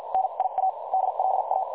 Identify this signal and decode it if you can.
Morse code, heard as a keyed tone, about 50 wpm, spaced as for 41 wpm, 760 Hz, with noise. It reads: NIA DLA
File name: dlazaawansowanych / nagraniazpasma / websdr recording 3565.0kHz